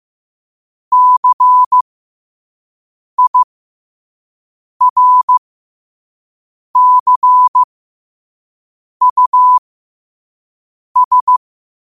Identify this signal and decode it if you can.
Morse code, heard as a keyed tone, about 15 wpm, spaced as for 5 wpm, 1000 Hz. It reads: CIRCUS
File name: qotdfiles / 5wpm